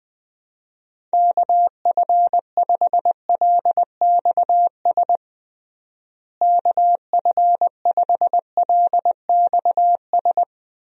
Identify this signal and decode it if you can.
Morse code, heard as a keyed tone, about 20 wpm, 700 Hz. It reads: KF5LXS KF5LXS
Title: USA Call Signs - 6 Characters - Repeated 2x - 20wpm